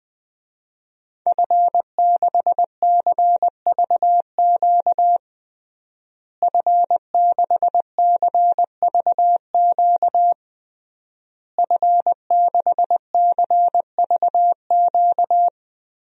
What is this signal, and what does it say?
Morse code, heard as a keyed tone, about 21 wpm, 700 Hz. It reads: F6CVQ F6CVQ F6CVQ
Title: DXCC-100 Call Signs - 5 Characters - Repeated 3x - 20wpm